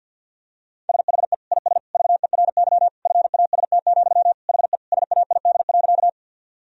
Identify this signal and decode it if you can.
Morse code, heard as a keyed tone, about 50 wpm, 700 Hz. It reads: SHE IS VERY FAST, HE SAID.
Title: Sentences from Top 1000 Words - Review Part 2 - 50wpm